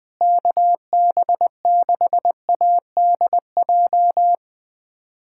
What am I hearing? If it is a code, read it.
Morse code, heard as a keyed tone, about 20 wpm, 700 Hz. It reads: KB6ADJ
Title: USA Callsign DE Callsign 20wpm